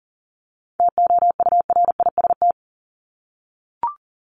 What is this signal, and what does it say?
Morse code, heard as a keyed tone, about 40 wpm, 700 Hz. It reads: TOURIST
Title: Top 1501-1750 Words 40wpm